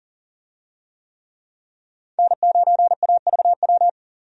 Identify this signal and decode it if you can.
Morse code, heard as a keyed tone, about 40 wpm, 700 Hz. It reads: N9AVW
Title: USA Call Signs - 5 Characters - 40wpm